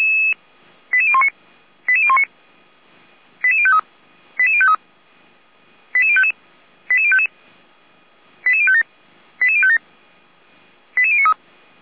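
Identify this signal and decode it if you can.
Five-tone selective call (ZVEI), heard as quick runs of five tones, tones of about 70 ms. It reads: 88019 88019 88053 88053 88055 88055 88057 88057 88093
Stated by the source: ZVEI five-tone selective call 89324 received by SDR.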